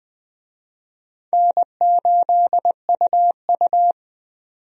Morse code, heard as a keyed tone, about 20 wpm, 700 Hz. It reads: N8UU
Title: USA Call Signs - 4 Characters - 20wpm